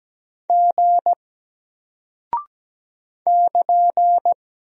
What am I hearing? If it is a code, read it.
Morse code, heard as a keyed tone, about 17 wpm, 700 Hz. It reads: G <KN>
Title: Single Character (Letter:Number:Punctuation:Prosign) Alt 17wpm